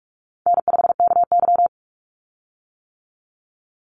Morse code, heard as a keyed tone, about 46 wpm, 700 Hz. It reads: N5KY
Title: Easy Call Signs - 45wpm